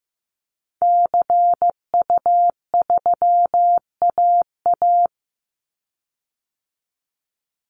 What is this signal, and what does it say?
Morse code, heard as a keyed tone, about 15 wpm, 700 Hz. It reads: CU3AA
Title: Call Signs Repeated 3x - 15wpm